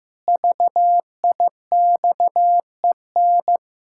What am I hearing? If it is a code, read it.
Morse code, heard as a keyed tone, about 16 wpm, 700 Hz. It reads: VIXEN
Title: Pangram Challenge v2 - 15wpm